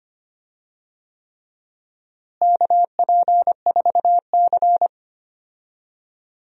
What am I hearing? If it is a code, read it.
Morse code, heard as a keyed tone, about 25 wpm, 700 Hz. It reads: KP4C
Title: DXCC-100 Call Signs - 4 Characters - Repeated 3x - 25wpm